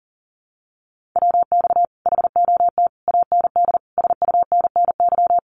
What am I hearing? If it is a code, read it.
Morse code, heard as a keyed tone, about 40 wpm, 700 Hz. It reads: WX HOT AND SUNNY
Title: Common QSO Phrases Encore 40wpm